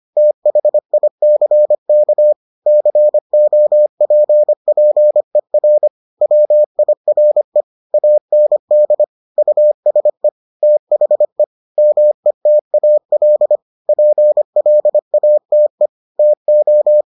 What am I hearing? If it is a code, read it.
Morse code, heard as a keyed tone, about 25 wpm, 600 Hz. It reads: THICK COPPER WIRE AND USE THE METAL PLATE TO